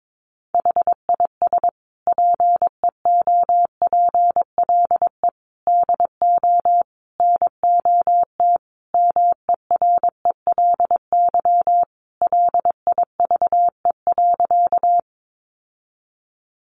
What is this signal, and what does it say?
Morse code, heard as a keyed tone, about 22 wpm, 700 Hz. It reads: HIS PEOPLE DO NOT MERELY LIVE.
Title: Sentences from Top 1400 Words 22wpm